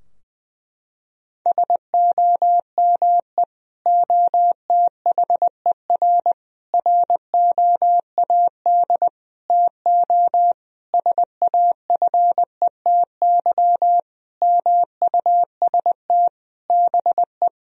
Morse code, heard as a keyed tone, about 20 wpm, 700 Hz. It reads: SOME OTHER ROAD TO SAFETY MUST BE
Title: Sentences from Top 1000 Words - Review Part 4 - 20wpm